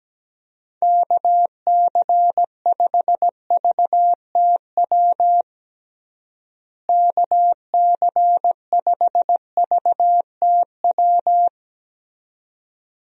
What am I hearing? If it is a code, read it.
Morse code, heard as a keyed tone, about 17 wpm, 700 Hz. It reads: KC5VTW KC5VTW
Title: USA Call Signs - 6 Characters - Repeated 2x - 17wpm